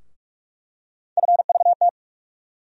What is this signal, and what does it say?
Morse code, heard as a keyed tone, about 46 wpm, 700 Hz. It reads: FVT
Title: USA Call Sign Postfixes - 3 Letters 45wpm